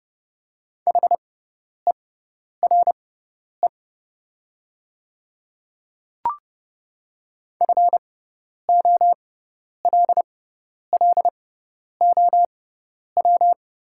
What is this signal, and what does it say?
Morse code, heard as a keyed tone, about 30 wpm, spaced as for 10 wpm, 700 Hz. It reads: HERE FOLLOW
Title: 6x Character Spacing - Top 200 Words - 30wpm